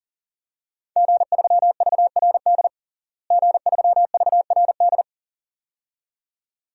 Morse code, heard as a keyed tone, about 40 wpm, 700 Hz. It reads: G3VRD G3VRD
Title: DXCC-100 Call Signs - 5 Characters - Repeated 2x - 40wpm